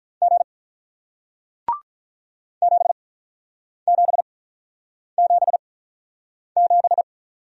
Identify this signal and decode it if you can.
Morse code, heard as a keyed tone, about 48 wpm, spaced as for 9 wpm, 700 Hz. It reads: G 7777